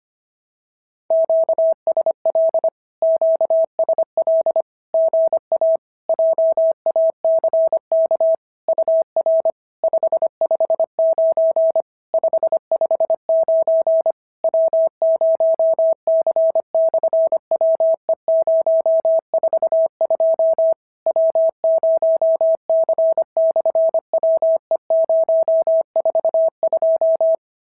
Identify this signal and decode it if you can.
Morse code, heard as a keyed tone, about 24 wpm, 650 Hz. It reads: QSL QSL GA JACK UR 559 559 W0C/WE042 W0C/WE042
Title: SOTA QSO Practice 25wpm